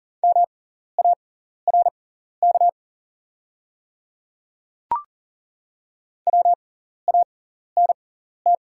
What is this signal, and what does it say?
Morse code, heard as a keyed tone, about 40 wpm, spaced as for 14 wpm, 700 Hz. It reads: MARK WANT
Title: 6x Character Spacing - Top 300 Words - 40wpm